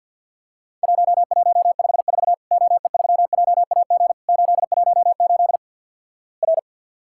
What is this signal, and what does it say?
Morse code, heard as a keyed tone, about 50 wpm, 700 Hz. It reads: 1154 OE3JAG 818 R
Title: Worked All Europe DX Contest - QTC Exchange - 50wpm